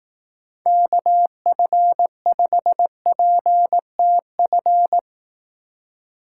Morse code, heard as a keyed tone, about 18 wpm, 700 Hz. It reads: KF5PTF
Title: Speed Racing - USA 2x3 Call Signs 15wpm